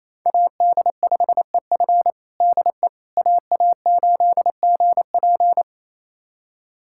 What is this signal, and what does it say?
Morse code, heard as a keyed tone, about 28 wpm, 700 Hz. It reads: AD5EF DE AA8GP
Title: USA Callsign DE Callsign - 5 Characters - 28wpm